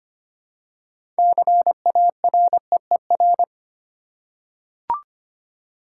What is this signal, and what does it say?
Morse code, heard as a keyed tone, about 25 wpm, 700 Hz. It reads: CAREER
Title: Six Letter Words - 25wpm